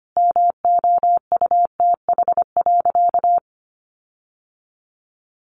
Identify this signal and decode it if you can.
Morse code, heard as a keyed tone, about 25 wpm, 700 Hz. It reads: MOUTH.
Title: Sentences from Top 1500 Words 25wpm